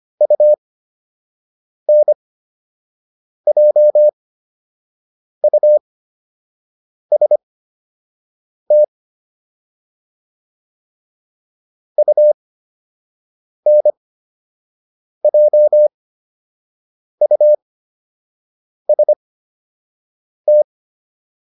Morse code, heard as a keyed tone, about 25 wpm, spaced as for 6 wpm, 600 Hz. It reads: UNJUST UNJUST